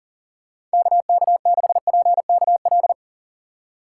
Morse code, heard as a keyed tone, about 39 wpm, 700 Hz. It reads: KK6PKL
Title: USA Call Signs - 6 Characters - 6x Character Spacing - 40wpm